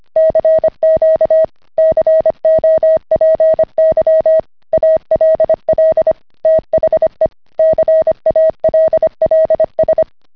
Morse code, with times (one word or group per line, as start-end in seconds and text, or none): 0.2-1.4 CQ
1.8-4.4 COPY
4.7-6.1 ALL
6.5-7.3 THE
7.6-10.0 CALLS